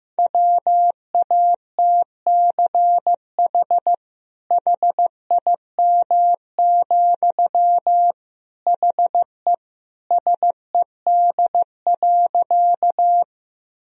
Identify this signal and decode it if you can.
Morse code, heard as a keyed tone, about 15 wpm, 700 Hz. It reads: WATCH HIM, HE SED.
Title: Sentences from Top 300 Words and Abbreviations 15wpm